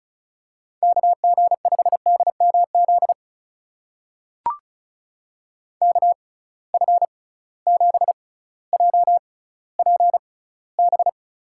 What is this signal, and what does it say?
Morse code, heard as a keyed tone, about 36 wpm, spaced as for 18 wpm, 700 Hz. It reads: KG5DMZ KF7JPB
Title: USA Call Signs - 6 Characters - 6x Character Spacing - 35wpm